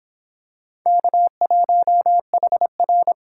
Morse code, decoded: K1HR

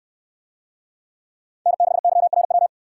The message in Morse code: N6YRW

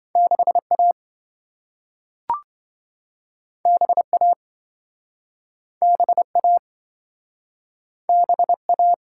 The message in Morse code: 6A BA BA BA

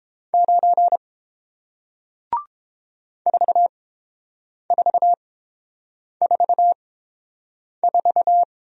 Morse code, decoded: 9 4444

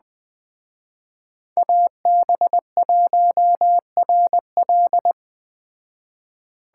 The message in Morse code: AB1RL

